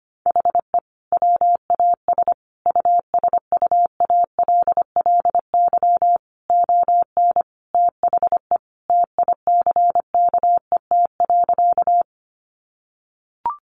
Morse code, heard as a keyed tone, about 25 wpm, 700 Hz. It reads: HE WAS USUALLY ON THE TICKET.
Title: Sentences from Top 1400 Words 25wpm